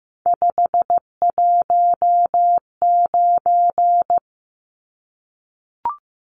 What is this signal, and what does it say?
Morse code, heard as a keyed tone, about 15 wpm, 700 Hz. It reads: 519
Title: Numbers - 3-Digit 15wpm